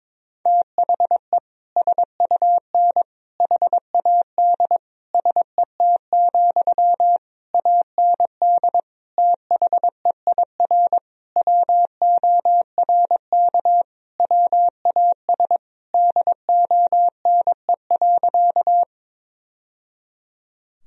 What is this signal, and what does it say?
Morse code, heard as a keyed tone, about 22 wpm, 700 Hz. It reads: THE SUN HAD SET, AND THEIR WORK WAS DONE.
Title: Sentences from Top 1000 Words - Review Part 1 - 22wpm